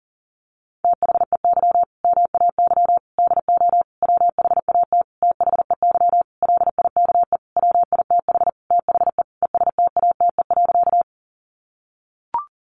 Morse code, heard as a keyed tone, about 40 wpm, 700 Hz. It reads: THEY MAY DO WHAT THEY LIKE WITH THE ESTATE.
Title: Sentences from Top 1200 Words 40wpm